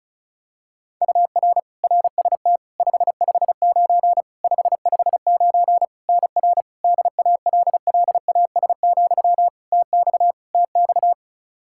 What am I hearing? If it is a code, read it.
Morse code, heard as a keyed tone, about 34 wpm, 700 Hz. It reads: UR RST 559 559 NR DALLAS, TX TX
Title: Practice QSO - Exchange 1 - Beginning - 35wpm